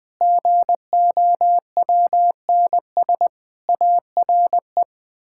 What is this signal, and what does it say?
Morse code, heard as a keyed tone, about 20 wpm, 700 Hz. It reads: GOWNS ARE
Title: Pangram Challenge v2 - 20wpm